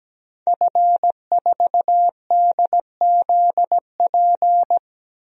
Morse code, heard as a keyed tone, about 17 wpm, 700 Hz. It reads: F4DZP
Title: DXCC-100 Call Signs - 5 Characters - 17wpm